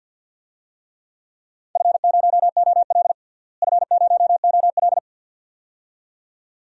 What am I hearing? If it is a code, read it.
Morse code, heard as a keyed tone, about 49 wpm, 700 Hz. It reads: F0OL F0OL
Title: DXCC-100 Call Signs - 4 Characters - Repeated 2x - 50wpm